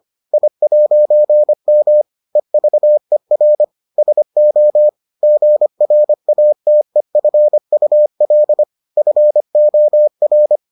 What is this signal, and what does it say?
Morse code, heard as a keyed tone, about 25 wpm, 600 Hz. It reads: I'M EVER SO GRATEFUL FOR